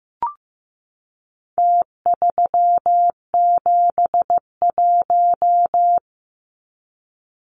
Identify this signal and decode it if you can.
Morse code, heard as a keyed tone, about 15 wpm, 700 Hz. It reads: T371
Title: Numbers - Contest Style - 4-Digit 15wpm